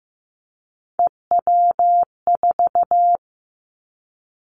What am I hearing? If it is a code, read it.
Morse code, heard as a keyed tone, about 15 wpm, 700 Hz. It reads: EW4
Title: Callsign Prefixes for Top 100 DXCC - 15wpm